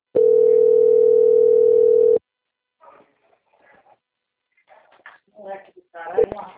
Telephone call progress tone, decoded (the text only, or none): ringback tone